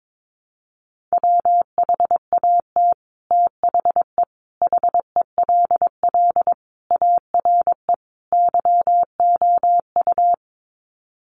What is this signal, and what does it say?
Morse code, heard as a keyed tone, about 22 wpm, 700 Hz. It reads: WHAT THE HELL ARE YOU
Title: Sets of 5 Words Encore 22wpm